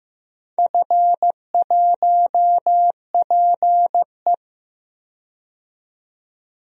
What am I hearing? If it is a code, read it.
Morse code, heard as a keyed tone, about 15 wpm, 700 Hz. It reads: F1PE
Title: DXCC-100 Call Signs - 4 Characters - 15wpm